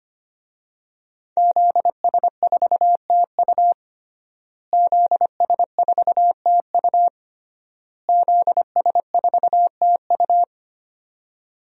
Morse code, heard as a keyed tone, about 25 wpm, 700 Hz. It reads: ZS4TU ZS4TU ZS4TU
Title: DXCC-100 Call Signs - 5 Characters - Repeated 3x - 25wpm